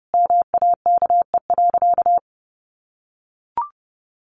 Morse code, heard as a keyed tone, about 30 wpm, 700 Hz. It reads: MAKE.
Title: Sentences from Top 1400 Words 30wpm